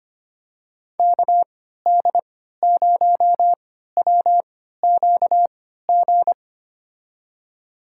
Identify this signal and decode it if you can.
Morse code, heard as a keyed tone, about 25 wpm, spaced as for 14 wpm, 700 Hz. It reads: KD0WQG
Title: USA Call Signs - 6 Characters - 3x Character Spacing - 25wpm